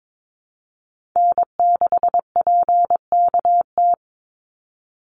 Morse code, heard as a keyed tone, about 22 wpm, 700 Hz. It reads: N6PKT
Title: NA SOTA Chaser Callsigns 22wpm